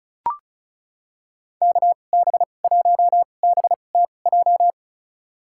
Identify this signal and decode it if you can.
Morse code, heard as a keyed tone, about 36 wpm, spaced as for 25 wpm, 700 Hz. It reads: KB1BTJ
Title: USA Call Signs - 6 Characters - 2x Character Spacing - 35wpm